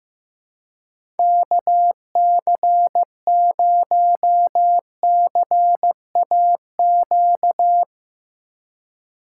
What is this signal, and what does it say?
Morse code, heard as a keyed tone, about 15 wpm, 700 Hz. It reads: KC0CAQ
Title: USA Call Signs - 6 Characters - 3x Character Spacing - 15wpm